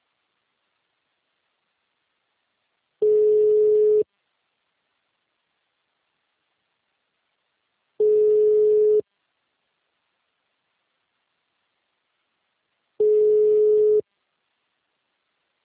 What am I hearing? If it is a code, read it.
Ringback tone (Europe, 425 Hz).